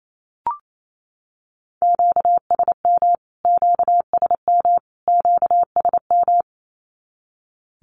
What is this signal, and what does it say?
Morse code, heard as a keyed tone, about 28 wpm, 700 Hz. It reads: QSM QSM QSM
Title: Q-Codes Repeated 3x 28wpm